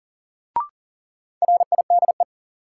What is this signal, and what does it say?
Morse code, heard as a keyed tone, about 40 wpm, 700 Hz. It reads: RIDE
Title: Sentences from Top 1000 Words - Review Part 6 - New Word Drill - 40wpm